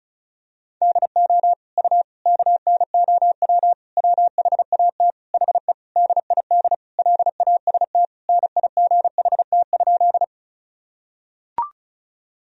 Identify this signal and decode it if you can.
Morse code, heard as a keyed tone, about 36 wpm, 700 Hz. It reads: DO U KNOW WHAT HE DID LAST NIGHT?
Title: Sentences from Top 300 Words and Abbreviations 35wpm